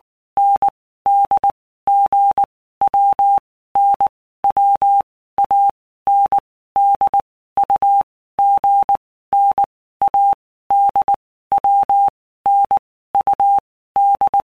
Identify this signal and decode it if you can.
Morse code, heard as a keyed tone, about 19 wpm, 800 Hz. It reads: N D G W N W A N D U G N A D W N U D
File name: Basic Letter Groups By Blocks / Groups@18 / AWUNDG@18